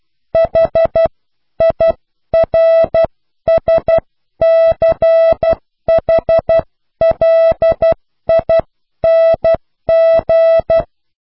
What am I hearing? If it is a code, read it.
Morse code, heard as a keyed tone, about 12 wpm, spaced as for 10 wpm, 660 Hz. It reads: HIRSCHLING
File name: cw / pseudo worden(10)